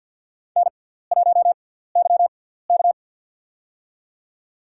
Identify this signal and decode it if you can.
Morse code, heard as a keyed tone, about 48 wpm, spaced as for 17 wpm, 700 Hz. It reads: N1YK
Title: USA Call Signs - 4 Characters - 6x Character Spacing - 50wpm